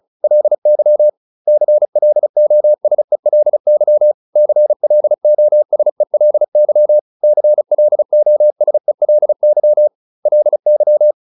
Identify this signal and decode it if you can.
Morse code, heard as a keyed tone, about 34 wpm, 600 Hz. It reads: LY CLOSELY CLOSELY CLOSELY LY
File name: LY7 35